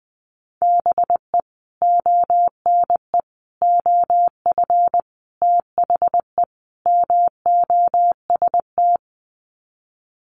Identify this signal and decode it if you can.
Morse code, heard as a keyed tone, about 20 wpm, 700 Hz. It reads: BE ONE OF THE MOST
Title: Sets of 5 Words Encore 20wpm